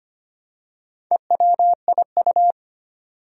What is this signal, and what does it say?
Morse code, heard as a keyed tone, about 25 wpm, 700 Hz. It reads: EWIU
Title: Four Challenging Letter-Number Combinations 25wpm